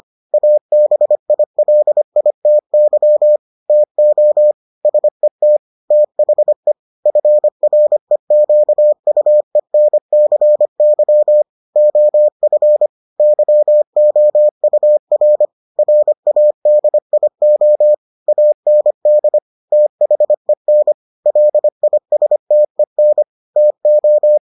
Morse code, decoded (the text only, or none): ABILITY TO SET THE FREQUENCY OF YOUR RADIO AND THEN LISTEN TO